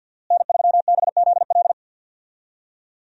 Morse code, heard as a keyed tone, about 52 wpm, 700 Hz. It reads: N3BZL
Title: USA Call Signs - 5 Characters - Repeated 3x - 50wpm